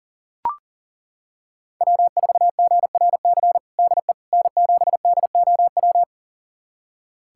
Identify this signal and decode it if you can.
Morse code, heard as a keyed tone, about 40 wpm, 700 Hz. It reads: W4GRC DE N7DOW
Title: USA Callsign DE Callsign - 5 Characters - 40wpm